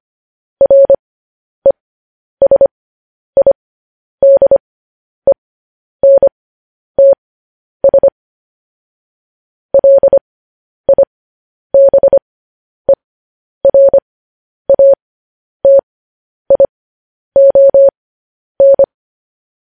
Morse code, heard as a keyed tone, about 25 wpm, spaced as for 10 wpm, 550 Hz. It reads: RESIDENTS LIBERATION
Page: news headlines in morse code at 10 wpm